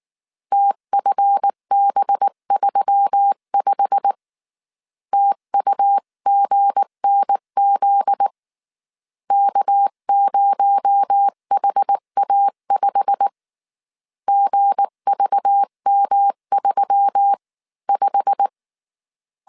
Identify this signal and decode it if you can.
Morse code, heard as a keyed tone, about 19 wpm, 800 Hz. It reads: TFB35 TUGNZ K0HA5 GVM35
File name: CW / DET / 1 9 1500C S B 18wpm 2551